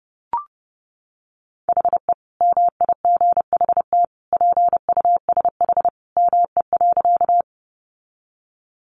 Morse code, heard as a keyed tone, about 30 wpm, 700 Hz. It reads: HE MIGHT PUSH ME.
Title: Sentences from Top 1300 Words 30wpm